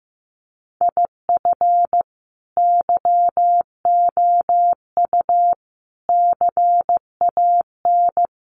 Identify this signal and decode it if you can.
Morse code, heard as a keyed tone, about 15 wpm, 700 Hz. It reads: IF YOU CAN
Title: Sets of 4 Words 15wpm